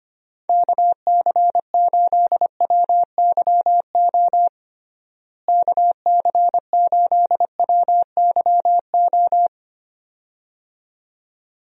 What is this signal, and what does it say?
Morse code, heard as a keyed tone, about 25 wpm, 700 Hz. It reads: KC8WYO KC8WYO
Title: USA Call Signs - 6 Characters - Repeated 2x - 25wpm